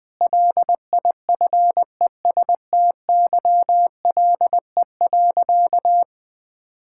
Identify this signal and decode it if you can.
Morse code, heard as a keyed tone, about 21 wpm, 700 Hz. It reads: LIFESTYLE.
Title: Difficult Sentences with Hints - 20wpm